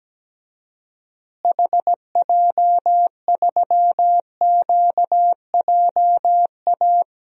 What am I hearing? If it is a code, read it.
Morse code, heard as a keyed tone, about 17 wpm, 700 Hz. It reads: HJ3QJA
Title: DXCC-100 Call Signs - 6 Characters - Repeated 2x - 17wpm